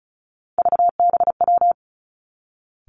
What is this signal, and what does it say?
Morse code, heard as a keyed tone, about 36 wpm, 700 Hz. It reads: VBW
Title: Three Challenging Letter-Number Combinations 35wpm